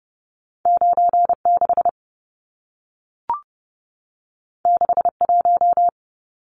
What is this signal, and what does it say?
Morse code, heard as a keyed tone, about 30 wpm, 700 Hz. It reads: 96 61
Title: Numbers - 2-Digit 30wpm